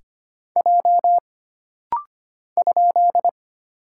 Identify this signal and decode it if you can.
Morse code, heard as a keyed tone, about 25 wpm, 700 Hz. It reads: J ?
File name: Single Character (Letter:Number:Punctuation:Prosign) - Rapid-Fire - 25wpm